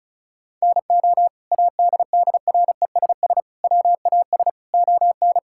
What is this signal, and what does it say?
Morse code, heard as a keyed tone, about 34 wpm, 700 Hz. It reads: NO ADDRESS WAS ON
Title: Sentences from Top 1000 Words - Review Part 6 - 35wpm